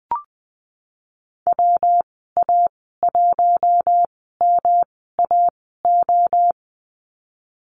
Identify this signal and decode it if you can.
Morse code, heard as a keyed tone, about 19 wpm, spaced as for 14 wpm, 700 Hz. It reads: WA1MAO
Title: 161 Lesson 26g - Letter g - Call Signs 20wpm